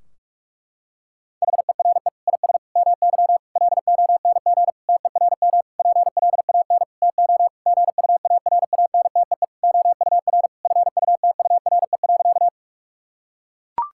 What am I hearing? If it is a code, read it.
Morse code, heard as a keyed tone, about 42 wpm, 700 Hz. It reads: HERE IS MY LONG TERM PLAN TO GUARANTEE OUR FUTURE.
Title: Sentences from Top 1600 Words 45wpm